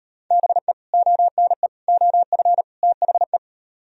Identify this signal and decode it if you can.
Morse code, heard as a keyed tone, about 37 wpm, 700 Hz. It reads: BE ONE OF THE